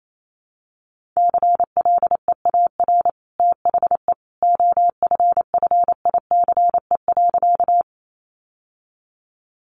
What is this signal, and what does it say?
Morse code, heard as a keyed tone, about 28 wpm, 700 Hz. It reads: CLEAR THE OFFICE.
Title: Sentences from Top 1000 Words 28wpm